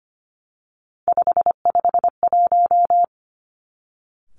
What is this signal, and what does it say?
Morse code, heard as a keyed tone, about 25 wpm, 700 Hz. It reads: 551
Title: Three Challenging Letter-Number Combinations 25wpm